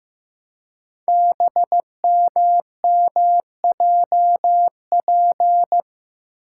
Morse code, heard as a keyed tone, about 15 wpm, 700 Hz. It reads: BMMJP
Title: Code Groups - 5 Letter:Numbers - 15wpm